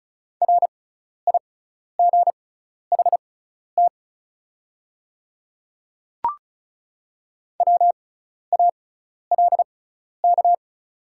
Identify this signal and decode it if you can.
Morse code, heard as a keyed tone, about 34 wpm, spaced as for 12 wpm, 690 Hz. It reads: RIGHT WALK
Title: 6x Character Spacing - Top 300 Words - 35wpm